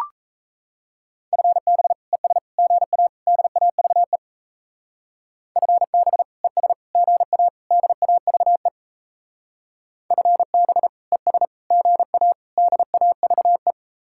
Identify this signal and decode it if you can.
Morse code, heard as a keyed tone, about 36 wpm, 700 Hz. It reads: FB ES GA DAVE FB ES GA DAVE FB ES GA DAVE